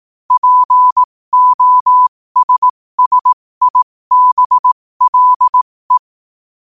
Morse code, heard as a keyed tone, about 18 wpm, spaced as for 15 wpm, 1000 Hz. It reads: POSSIBLE
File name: qotdfiles / 15wpm